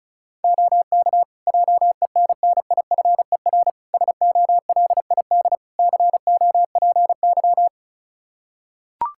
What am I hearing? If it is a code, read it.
Morse code, heard as a keyed tone, about 34 wpm, 700 Hz. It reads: OK JENNIFER SOLID COPY
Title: Practice QSO - Exchange 3 - More Info - 35wpm